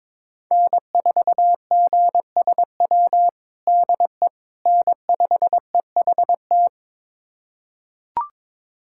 Morse code, heard as a keyed tone, about 22 wpm, 700 Hz. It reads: N4GSW DE N5EHT
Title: USA Callsign DE Callsign - 5 Characters - 22wpm